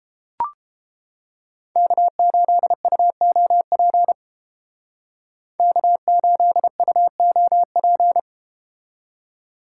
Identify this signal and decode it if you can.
Morse code, heard as a keyed tone, about 31 wpm, 700 Hz. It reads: K8UOP K8UOP